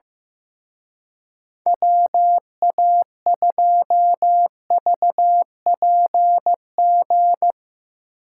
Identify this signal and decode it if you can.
Morse code, heard as a keyed tone, about 16 wpm, 700 Hz. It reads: WA2VPG
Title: Speed Racing - USA 2x3 Call Signs 15wpm